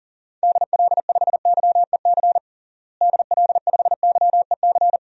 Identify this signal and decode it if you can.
Morse code, heard as a keyed tone, about 40 wpm, 700 Hz. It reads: DL5YEC DL5YEC
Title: DXCC-100 Call Signs - 6 Characters - Repeated 2x - 40wpm